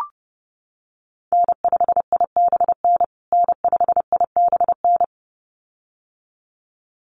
Morse code, heard as a keyed tone, about 30 wpm, 700 Hz. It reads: N5IBN N5IBN
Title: USA Call Signs Repeated 2x 30wpm